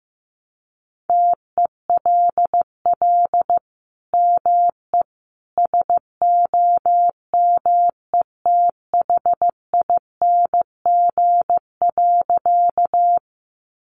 Morse code, heard as a keyed tone, about 15 wpm, 700 Hz. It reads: TELL ME SOMETHING.
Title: Sentences from Top 600 Words 15wpm